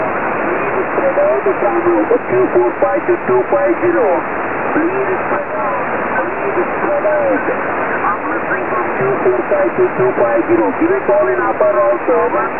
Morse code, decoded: none